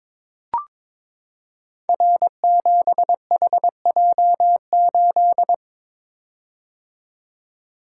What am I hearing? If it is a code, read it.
Morse code, heard as a keyed tone, about 22 wpm, 700 Hz. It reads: R7HJ8
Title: Code Groups - 5 Letter:Numbers - 22wpm